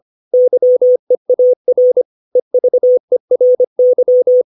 Morse code, heard as a keyed tone, about 25 wpm, 500 Hz. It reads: YEAR EVERY
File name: 632 words / 632 Words 25WPM